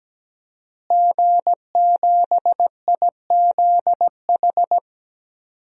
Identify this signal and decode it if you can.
Morse code, heard as a keyed tone, about 17 wpm, 700 Hz. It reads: G7IZH
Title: DXCC-100 Call Signs - 5 Characters - Repeated 2x - 17wpm